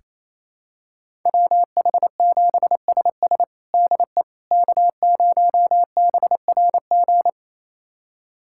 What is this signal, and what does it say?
Morse code, heard as a keyed tone, about 28 wpm, 700 Hz. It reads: WH7SS DE K0BRG
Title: USA Callsign DE Callsign - 5 Characters - 28wpm